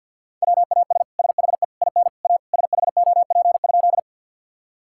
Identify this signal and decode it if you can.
Morse code, heard as a keyed tone, about 50 wpm, 700 Hz. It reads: WAS SHE IN A SHOP?
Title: Sentences from Top 1600 Words 50wpm